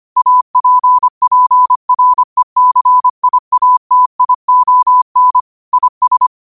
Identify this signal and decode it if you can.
Morse code, heard as a keyed tone, about 25 wpm, 1000 Hz. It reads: APPRECIATION IS
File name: qotdfiles / 25wpm